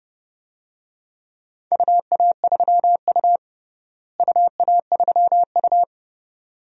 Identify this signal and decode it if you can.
Morse code, heard as a keyed tone, about 30 wpm, 700 Hz. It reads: UA3U UA3U
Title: DXCC-100 Call Signs - 4 Characters - Repeated 2x - 30wpm